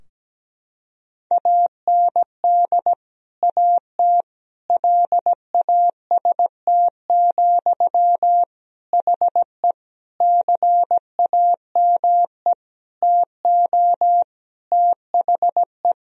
Morse code, heard as a keyed tone, about 17 wpm, 700 Hz. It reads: AND AT LAST, HE CAME TO THE
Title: Sentences from Top 1000 Words - Review Part 2 - 17wpm